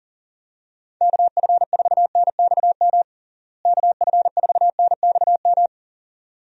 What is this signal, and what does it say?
Morse code, heard as a keyed tone, about 37 wpm, 700 Hz. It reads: KF4NXM KF4NXM
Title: USA Call Signs - 6 Characters - Repeated 2x - 40wpm